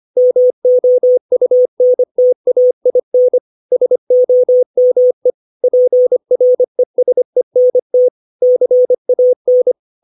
Morse code, heard as a keyed tone, about 25 wpm, 500 Hz. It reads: MOUNTAIN SOME PRESENT CAN
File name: 632 words / 632 Words 25WPM